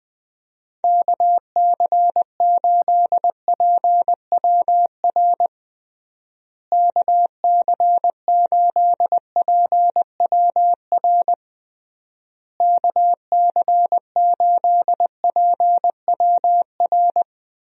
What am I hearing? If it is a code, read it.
Morse code, heard as a keyed tone, about 20 wpm, 700 Hz. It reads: KC8PWR KC8PWR KC8PWR
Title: USA Call Signs - 6 Characters - Repeated 3x - 20wpm